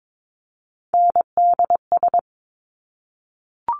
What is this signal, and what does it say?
Morse code, heard as a keyed tone, about 22 wpm, 700 Hz. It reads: NDS
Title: Top Three Letter Combinations in Words 22wpm